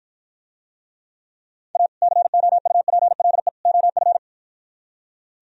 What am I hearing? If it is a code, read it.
Morse code, heard as a keyed tone, about 52 wpm, 700 Hz. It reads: A COUPLE OF